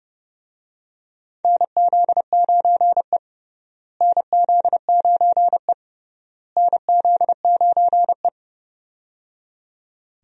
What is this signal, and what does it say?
Morse code, heard as a keyed tone, about 30 wpm, 700 Hz. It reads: NZ9E NZ9E NZ9E
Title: USA Call Signs - 4 Characters - Repeated 3x - 30wpm